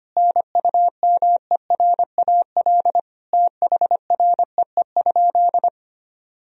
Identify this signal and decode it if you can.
Morse code, heard as a keyed tone, about 25 wpm, 700 Hz. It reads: NUMERAL THREE?
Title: Sentences from Top 1000 Words - Review Part 2 - 25wpm